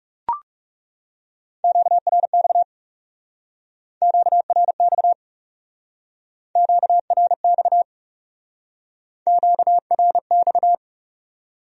Morse code, heard as a keyed tone, about 39 wpm, 700 Hz. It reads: QRX QRX QRX QRX